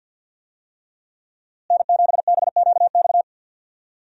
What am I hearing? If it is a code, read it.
Morse code, heard as a keyed tone, about 52 wpm, 700 Hz. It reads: N7BQX
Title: USA Call Signs - 5 Characters - Repeated 3x - 50wpm